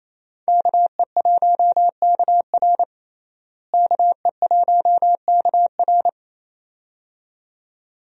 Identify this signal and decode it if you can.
Morse code, heard as a keyed tone, about 28 wpm, 700 Hz. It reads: KE1KR KE1KR
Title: USA Call Signs - 5 Characters - Repeated 2x - 28wpm